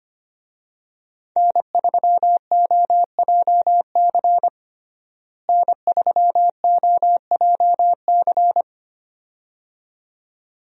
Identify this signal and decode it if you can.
Morse code, heard as a keyed tone, about 25 wpm, 700 Hz. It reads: N3OJC N3OJC
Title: USA Call Signs - 5 Characters - Repeated 2x - 25wpm